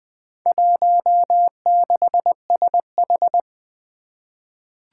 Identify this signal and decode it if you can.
Morse code, heard as a keyed tone, about 20 wpm, 700 Hz. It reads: 16SH
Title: Four Challenging Letter-Number Combinations 20wpm